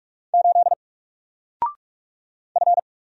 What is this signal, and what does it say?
Morse code, heard as a keyed tone, about 44 wpm, 700 Hz. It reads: 8 F
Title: Single Letter-Number 45wpm